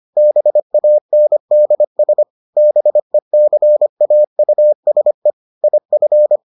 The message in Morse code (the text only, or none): BANDS BECAUSE IF